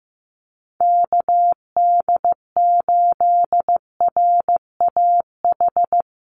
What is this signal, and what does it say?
Morse code, heard as a keyed tone, about 15 wpm, 700 Hz. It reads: KD8RAH
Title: USA Call Signs 15wpm